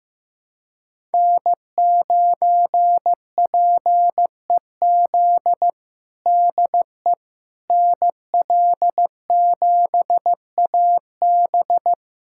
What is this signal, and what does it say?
Morse code, heard as a keyed tone, about 15 wpm, 700 Hz. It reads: N9PEZ DE NL7AB
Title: USA Callsign DE Callsign - 5 Characters - 15wpm